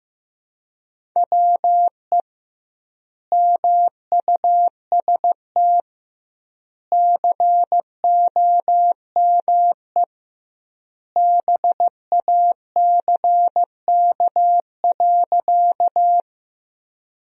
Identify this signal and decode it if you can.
Morse code, heard as a keyed tone, about 16 wpm, 700 Hz. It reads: WE MUST COME BACK.
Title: 2x Word-Spacing - Sentences from Top 200 Words - 15wpm